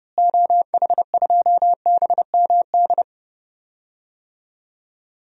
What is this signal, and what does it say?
Morse code, heard as a keyed tone, about 30 wpm, 700 Hz. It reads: OH2BMD
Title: DXCC-100 Call Signs - 6 Characters - Repeated 3x - 30wpm